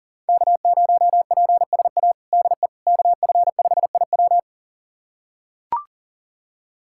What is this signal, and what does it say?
Morse code, heard as a keyed tone, about 40 wpm, 700 Hz. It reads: K0PSA DE KF5IW
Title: USA Callsign DE Callsign - 5 Characters - 40wpm